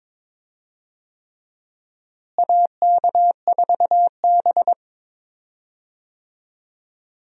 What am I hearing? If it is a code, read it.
Morse code, heard as a keyed tone, about 22 wpm, 700 Hz. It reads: AK4B
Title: USA Call Signs - 4 Characters - Repeated 3x - 22wpm